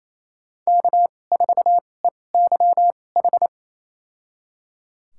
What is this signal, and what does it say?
Morse code, heard as a keyed tone, about 27 wpm, spaced as for 20 wpm, 700 Hz. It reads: K4EYH
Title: USA Call Signs - 5 Characters - 2x Character Spacing - 28wpm